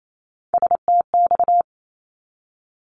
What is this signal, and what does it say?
Morse code, heard as a keyed tone, about 28 wpm, 700 Hz. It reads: STX
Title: ARRL Sections 28wpm